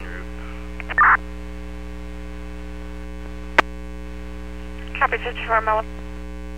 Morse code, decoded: none

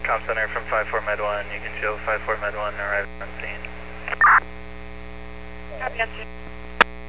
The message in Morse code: none

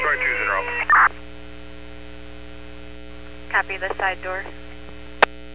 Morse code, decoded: none

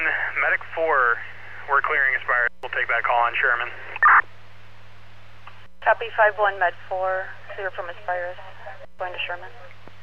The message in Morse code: none

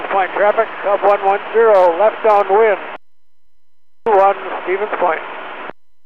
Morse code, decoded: none

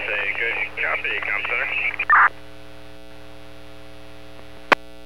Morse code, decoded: none

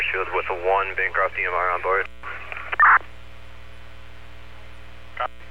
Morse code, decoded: none

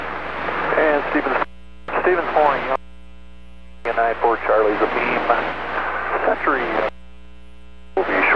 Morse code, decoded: none

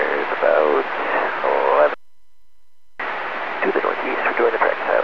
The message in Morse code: none

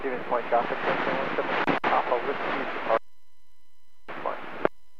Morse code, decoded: none

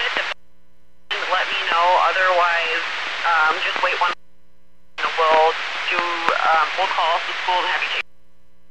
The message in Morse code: none

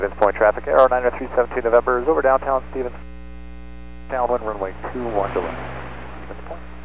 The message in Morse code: none

